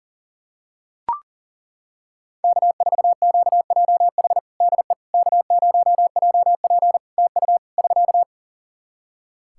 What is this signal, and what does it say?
Morse code, heard as a keyed tone, about 40 wpm, 700 Hz. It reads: K4QJH DE K0JP TU <SK>
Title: Practice QSO - Exchange 4 - Ending - 40wpm